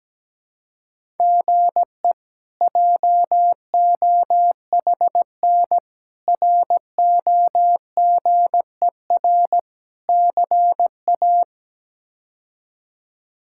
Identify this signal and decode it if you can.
Morse code, heard as a keyed tone, about 17 wpm, 700 Hz. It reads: GE JOHN ROGER CA
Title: K1USN SST QSO Practice - 17wpm